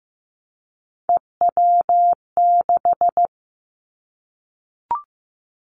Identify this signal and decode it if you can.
Morse code, heard as a keyed tone, about 15 wpm, 700 Hz. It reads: EW6
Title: Three Challenging Letter-Number Combinations 15wpm